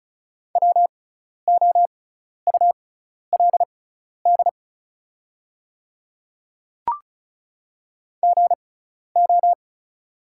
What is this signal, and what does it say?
Morse code, heard as a keyed tone, about 36 wpm, spaced as for 12 wpm, 700 Hz. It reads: WOULD GO
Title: 6x Character Spacing - Top 100 Words - 35wpm